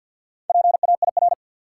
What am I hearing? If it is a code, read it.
Morse code, heard as a keyed tone, about 53 wpm, 700 Hz. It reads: PAIR
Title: Sentences from Top 1200 Words - New Word Drill Alt - 50wpm